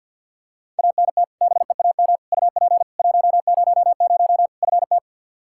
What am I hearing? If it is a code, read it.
Morse code, heard as a keyed tone, about 45 wpm, 700 Hz. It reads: ANT BEAM UP 100 FT
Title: K1USN SST QSO Practice 50wpm